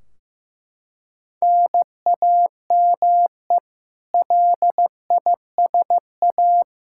Morse code, heard as a keyed tone, about 16 wpm, 700 Hz. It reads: NAME LISA
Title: Practice QSO - Exchange 1 - Beginning - 15wpm